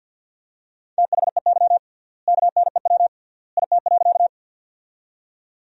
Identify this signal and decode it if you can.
Morse code, heard as a keyed tone, about 50 wpm, 700 Hz. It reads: THEY KNEW IT.
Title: 3x Word-Spacing - Sentences from Top 400 Words - 50wpm